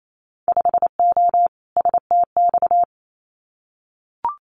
Morse code, heard as a keyed tone, about 28 wpm, 700 Hz. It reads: 5O STX
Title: Winter Field Day Exchanges 28wpm